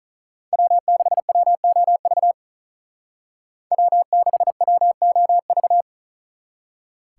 Morse code, heard as a keyed tone, about 44 wpm, 700 Hz. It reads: W6WOV W6WOV